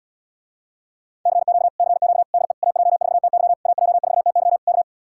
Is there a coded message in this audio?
Yes: Morse code from 1.3 to 4.8 s, heard as a keyed tone, about 77 wpm, 700 Hz. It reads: CQ CQ DE AJ4TY AJ4TY K